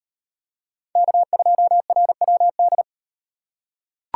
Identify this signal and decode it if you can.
Morse code, heard as a keyed tone, about 38 wpm, 700 Hz. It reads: K2RWD